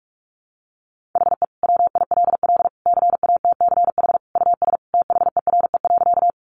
Morse code, heard as a keyed tone, about 44 wpm, 700 Hz. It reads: HE WILL CATCH US THERE.
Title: Sentences from Top 1500 Words 45wpm